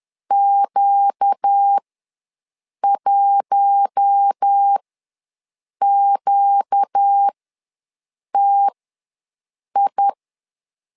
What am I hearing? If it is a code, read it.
Morse code, heard as a keyed tone, about 10 wpm, spaced as for 6 wpm, 800 Hz. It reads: Q1QTI
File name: LEZIONI / 4 Q1 ws 10wpm 2538